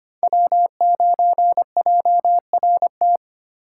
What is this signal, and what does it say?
Morse code, heard as a keyed tone, about 25 wpm, 700 Hz. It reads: W9JRT
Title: USA Call Signs - 5 Characters - 4x Character Spacing - 25wpm